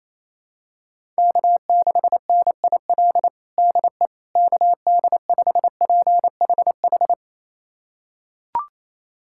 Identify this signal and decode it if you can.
Morse code, heard as a keyed tone, about 28 wpm, 700 Hz. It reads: K6NIL DE KD5PHH
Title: USA Callsign DE Callsign 28wpm